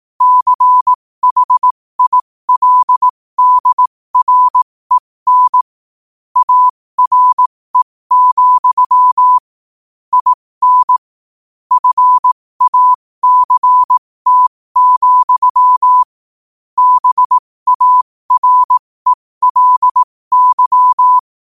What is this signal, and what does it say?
Morse code, heard as a keyed tone, about 18 wpm, spaced as for 15 wpm, 1000 Hz. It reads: CHILDREN ARE, IN FACT, BARELY
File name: qotdfiles / 15wpm